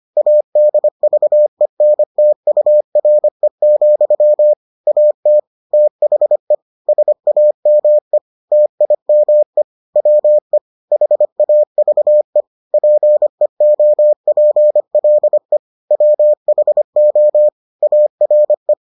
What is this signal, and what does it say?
Morse code, heard as a keyed tone, about 25 wpm, 600 Hz. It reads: ADVENTURE, AT THE SAME TIME WE HAVE PEOPLE WHO ARE